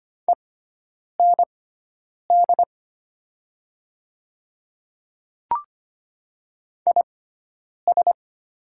Morse code, heard as a keyed tone, about 25 wpm, spaced as for 9 wpm, 700 Hz. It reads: END IS
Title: 6x Character Spacing - Top 200 Words - 25wpm